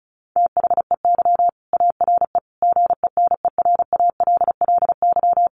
Morse code, heard as a keyed tone, about 34 wpm, 700 Hz. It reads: THEY ARE GENERALLY
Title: Sentences from Top 600 Words 35wpm